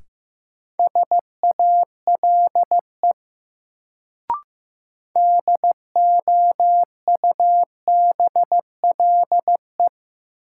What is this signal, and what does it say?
Morse code, heard as a keyed tone, about 15 wpm, 700 Hz. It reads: SALE DOUBLE
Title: Sentences from Top 1400 Words - New Word Drill - 15wpm